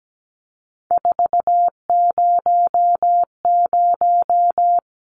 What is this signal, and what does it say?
Morse code, heard as a keyed tone, about 17 wpm, 700 Hz. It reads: 400
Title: Numbers - 3-Digit 17wpm